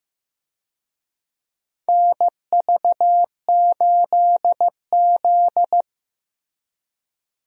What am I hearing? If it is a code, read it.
Morse code, heard as a keyed tone, about 15 wpm, 700 Hz. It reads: NV8Z
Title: USA Call Signs - 4 Characters - Repeated 3x - 15wpm